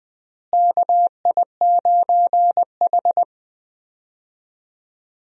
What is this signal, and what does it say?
Morse code, heard as a keyed tone, about 20 wpm, 700 Hz. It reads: KI9H
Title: USA Call Signs - 4 Characters - Repeated 2x - 20wpm